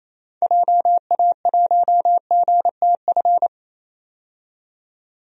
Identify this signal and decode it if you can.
Morse code, heard as a keyed tone, about 28 wpm, 700 Hz. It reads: JA1GTF
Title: DXCC-100 Call Signs - 6 Characters - 28wpm